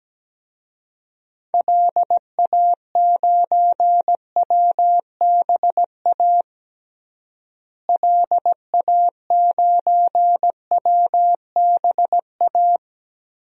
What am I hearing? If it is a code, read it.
Morse code, heard as a keyed tone, about 17 wpm, 700 Hz. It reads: LA9WBA LA9WBA
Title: DXCC-100 Call Signs - 6 Characters - Repeated 3x - 17wpm